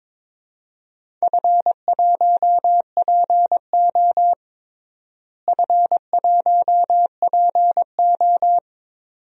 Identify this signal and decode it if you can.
Morse code, heard as a keyed tone, about 22 wpm, 700 Hz. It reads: F1PO F1PO
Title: DXCC-100 Call Signs - 4 Characters - Repeated 2x - 22wpm